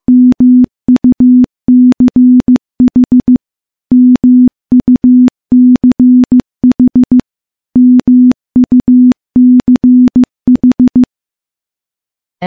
Morse code, heard as a keyed tone, about 15 wpm, 250 Hz. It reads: MUCH MUCH MUCH
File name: morse / morse-training-15WPM